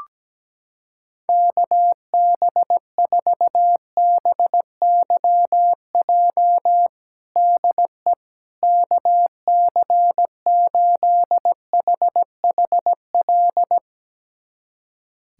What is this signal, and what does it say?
Morse code, heard as a keyed tone, about 17 wpm, 700 Hz. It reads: KB4BYJ DE KC8HHL
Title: USA Callsign DE Callsign 17wpm